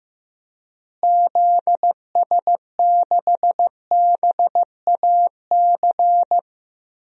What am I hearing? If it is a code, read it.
Morse code, heard as a keyed tone, about 15 wpm, 700 Hz. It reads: ZS6BAC
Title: DXCC-100 Call Signs - 6 Characters - Repeated 3x - 15wpm